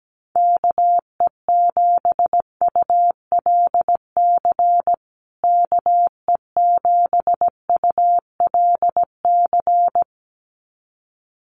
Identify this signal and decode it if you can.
Morse code, heard as a keyed tone, about 17 wpm, 700 Hz. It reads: KE7ULC KE7ULC
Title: USA Call Signs Repeated 2x 17wpm